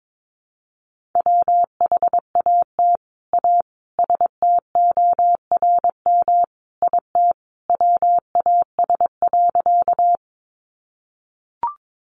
Morse code, heard as a keyed tone, about 22 wpm, 700 Hz. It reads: WHAT A STORM IT WAS.
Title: Sentences from Top 1500 Words 22wpm